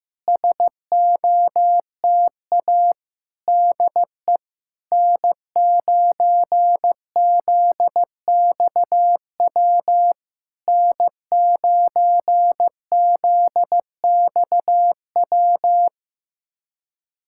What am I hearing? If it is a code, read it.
Morse code, heard as a keyed tone, about 15 wpm, 700 Hz. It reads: SOTA DE N9ZXW N9ZXW
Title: SOTA QSO Practice 15wpm